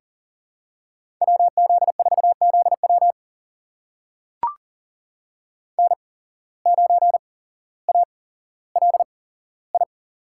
Morse code, heard as a keyed tone, about 40 wpm, spaced as for 17 wpm, 700 Hz. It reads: WZ4ZW N9ALI
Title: USA Call Signs - 5 Characters - 8x Character Spacing - 40wpm